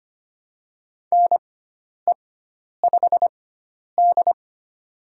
Morse code, heard as a keyed tone, about 25 wpm, spaced as for 10 wpm, 700 Hz. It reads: NE5D
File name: Lesson 16g - Letter c - Call Signs 25wpm